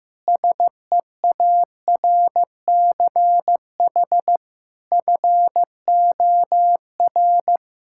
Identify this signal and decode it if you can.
Morse code, heard as a keyed tone, about 16 wpm, 700 Hz. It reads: SEARCH FOR
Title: Sentences from Top 1000 Words - Review Part 5 - 15wpm